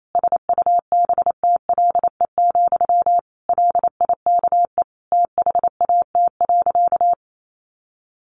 Morse code, heard as a keyed tone, about 28 wpm, 700 Hz. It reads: SUBTLE, LIKE THAT.
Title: A Sound Of Thunder 28wpm